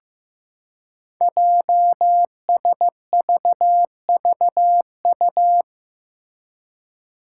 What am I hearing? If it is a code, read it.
Morse code, heard as a keyed tone, about 15 wpm, 700 Hz. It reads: JSVVU
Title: Five Challenging Letter-Number Combinations 15wpm